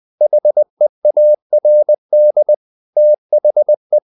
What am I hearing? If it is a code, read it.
Morse code, heard as a keyed tone, about 20 wpm, 600 Hz. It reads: HEARD THE
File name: the old man and the sea part 2 20wpm